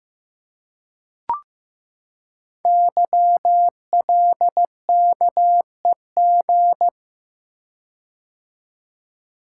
Morse code, heard as a keyed tone, about 15 wpm, 700 Hz. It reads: YLKEG
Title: Code Groups - 5 Letters - 15wpm